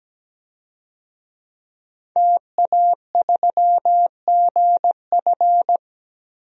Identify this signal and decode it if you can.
Morse code, heard as a keyed tone, about 17 wpm, 700 Hz. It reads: TA3GF
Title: DXCC-100 Call Signs - 5 Characters - Repeated 3x - 17wpm